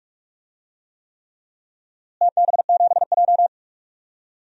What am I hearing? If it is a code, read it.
Morse code, heard as a keyed tone, about 45 wpm, 700 Hz. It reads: TB7J
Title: DXCC-100 Call Signs - 4 Characters - Repeated 3x - 45wpm